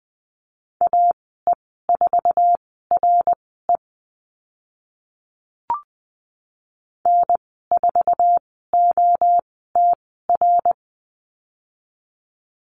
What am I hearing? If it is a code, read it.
Morse code, heard as a keyed tone, about 20 wpm, spaced as for 15 wpm, 700 Hz. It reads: AE4RE N4OTR
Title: 055 Lesson 10g - Letter r - Call Signs 20wpm